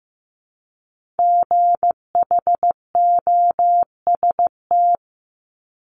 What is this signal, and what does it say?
Morse code, heard as a keyed tone, about 15 wpm, 700 Hz. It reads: GHOST
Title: Multi-letter Phonemes Repeated 3x 15wpm